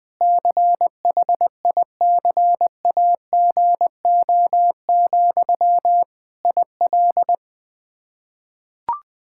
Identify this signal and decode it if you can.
Morse code, heard as a keyed tone, about 20 wpm, 700 Hz. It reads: CHICAGO, IL
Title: K1USN SST QSO Practice 20wpm